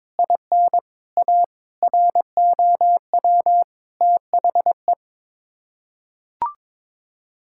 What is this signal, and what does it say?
Morse code, heard as a keyed tone, about 22 wpm, 700 Hz. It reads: IN A ROW THE
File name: practice / 22wpm / Sets of 8 Words - 22wpm